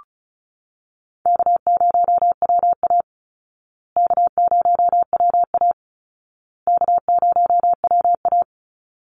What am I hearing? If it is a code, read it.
Morse code, heard as a keyed tone, about 34 wpm, 700 Hz. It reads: K0WA K0WA K0WA
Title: Easy Call Signs Repeated 3x - 35wpm